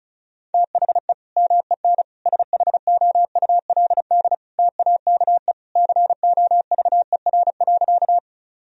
Morse code, heard as a keyed tone, about 36 wpm, 700 Hz. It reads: THE MEN SHOULD TAKE COVER.
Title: Sentences from Top 1000 Words - Review Part 1 - 35wpm